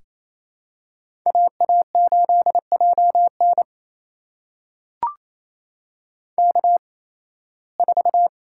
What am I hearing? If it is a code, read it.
Morse code, heard as a keyed tone, about 28 wpm, spaced as for 18 wpm, 700 Hz. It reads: AA8JN K4
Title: USA Call Signs - 5 Characters - 8x Character Spacing - 28wpm